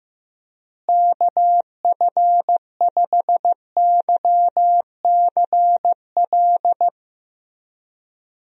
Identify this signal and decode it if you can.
Morse code, heard as a keyed tone, about 15 wpm, 700 Hz. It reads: KF5YCL
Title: USA Call Signs - 6 Characters - Repeated 3x - 15wpm